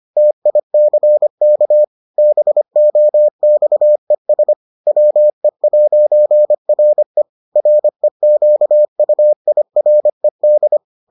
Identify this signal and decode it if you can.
Morse code, heard as a keyed tone, about 25 wpm, 600 Hz. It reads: TICK BOXES WE'RE REQUIRED